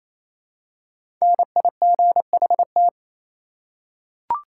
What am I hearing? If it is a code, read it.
Morse code, heard as a keyed tone, about 28 wpm, 700 Hz. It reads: NIGHT
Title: Five Letter Words - 28wpm